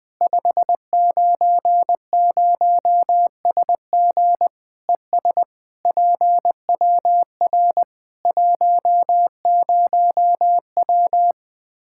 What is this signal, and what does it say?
Morse code, heard as a keyed tone, about 20 wpm, 700 Hz. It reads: 590SG ES PWR 10W
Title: Practice QSO - Exchange 2 - Info - 20wpm